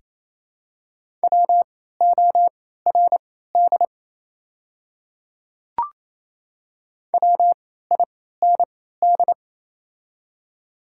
Morse code, heard as a keyed tone, about 28 wpm, spaced as for 16 wpm, 690 Hz. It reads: WORD WIND
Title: 3x Character Spacing - Top 500 Words - 28wpm